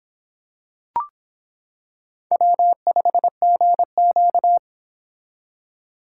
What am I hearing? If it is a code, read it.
Morse code, heard as a keyed tone, about 26 wpm, 700 Hz. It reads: W5GQ